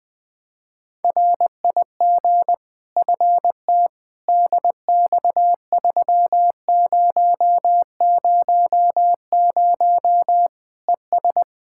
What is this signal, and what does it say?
Morse code, heard as a keyed tone, about 20 wpm, 700 Hz. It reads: RIG FT DX3000 ES
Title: K1USN SST QSO Practice 20wpm